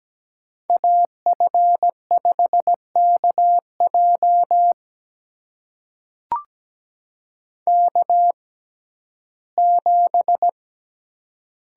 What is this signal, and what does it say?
Morse code, heard as a keyed tone, about 17 wpm, spaced as for 12 wpm, 700 Hz. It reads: AF5KJ K7
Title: USA Call Signs - 5 Characters - 6x Character Spacing - 17wpm